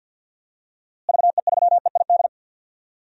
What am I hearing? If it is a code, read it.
Morse code, heard as a keyed tone, about 50 wpm, 700 Hz. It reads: VE3EID
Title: DXCC-100 Call Signs - 6 Characters - Repeated 3x - 50wpm